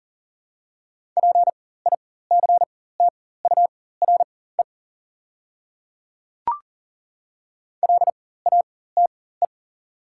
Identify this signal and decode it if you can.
Morse code, heard as a keyed tone, about 40 wpm, spaced as for 19 wpm, 700 Hz. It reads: PICTURE LATE
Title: 4x Character Spacing - Top 300 Words - 40wpm